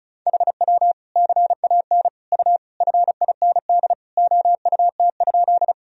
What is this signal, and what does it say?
Morse code, heard as a keyed tone, about 34 wpm, 700 Hz. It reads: HW CAN U FIND OUT?
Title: Sentences from Top 200 Words and Abbreviations 35wpm